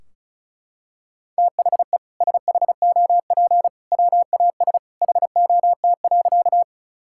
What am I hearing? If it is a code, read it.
Morse code, heard as a keyed tone, about 34 wpm, 700 Hz. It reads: THE SHOP WAS HOT.
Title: Sentences from Top 1600 Words 35wpm